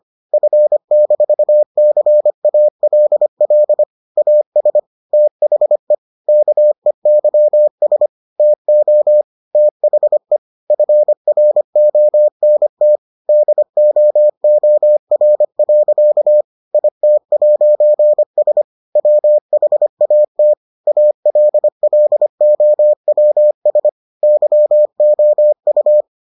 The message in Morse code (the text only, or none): F-CALL AS THE KEYS TO THE FRONT DOOR. IT'S WHAT ALLOWS YOU